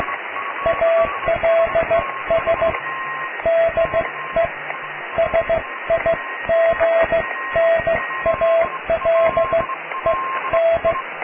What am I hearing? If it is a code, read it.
Morse code, heard as a keyed tone, about 15 wpm, spaced as for 13 wpm, 660 Hz, with radio noise. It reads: ALS DE SIGNALEN
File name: cw / tekst(a)+QRM